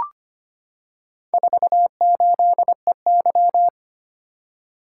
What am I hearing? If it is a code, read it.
Morse code, heard as a keyed tone, about 25 wpm, 700 Hz. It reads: 48EY
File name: practice / 25wpm / Four Letter-Number Combinations - 25wpm